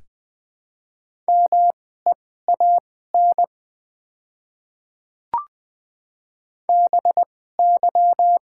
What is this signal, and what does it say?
Morse code, heard as a keyed tone, about 20 wpm, spaced as for 14 wpm, 700 Hz. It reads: MEAN BY
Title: 2x Character Spacing - Top 200 Words - 20wpm